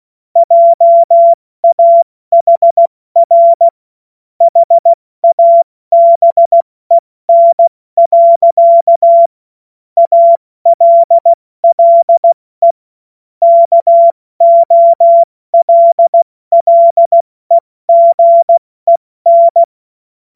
Morse code, heard as a keyed tone, about 16 wpm, 680 Hz. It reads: JAHR HABEN. ALLE KOLLEGEN
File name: Morsekurs / Witze / Witz 09. August 2022 70 F80